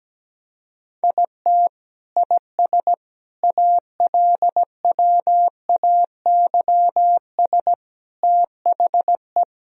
Morse code, heard as a keyed tone, about 17 wpm, 700 Hz. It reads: IT IS ALWAYS THE
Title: Sentences from Top 1000 Words - Review Part 1 - 17wpm